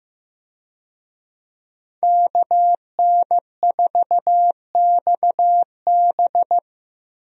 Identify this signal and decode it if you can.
Morse code, heard as a keyed tone, about 16 wpm, 700 Hz. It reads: KN4XB
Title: USA Call Signs - 5 Characters - Repeated 3x - 15wpm